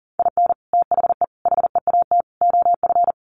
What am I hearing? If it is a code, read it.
Morse code, heard as a keyed tone, about 40 wpm, 700 Hz. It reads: IN THE HEAT OF
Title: Sets of 5 Words Encore2 40wpm